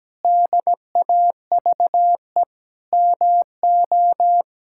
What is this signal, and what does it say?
Morse code, heard as a keyed tone, about 17 wpm, 700 Hz. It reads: DAVE MO
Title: K1USN SST QSO Practice - 17wpm